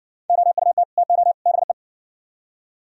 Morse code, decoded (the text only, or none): OUT TO BE